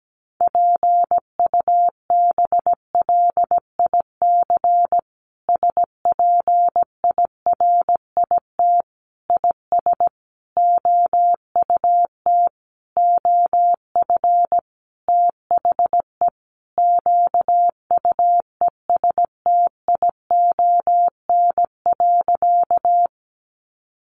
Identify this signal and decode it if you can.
Morse code, heard as a keyed tone, about 17 wpm, 700 Hz. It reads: PUBLIC SPIRIT IS OUT OF THE QUESTION.
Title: Sentences from Top 1400 Words 17wpm